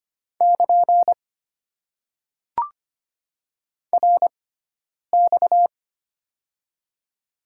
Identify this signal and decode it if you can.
Morse code, heard as a keyed tone, about 25 wpm, spaced as for 9 wpm, 700 Hz. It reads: <KN> RX
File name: Top 90 QSO Elements - 6x Character Spacing - 25wpm